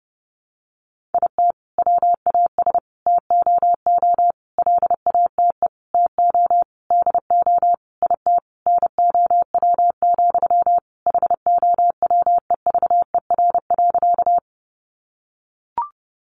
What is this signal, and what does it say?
Morse code, heard as a keyed tone, about 30 wpm, 700 Hz. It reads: IT WAS TOO LATE TO DO IT NOW, HOWEVER.
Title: Sentences from Top 600 Words 30wpm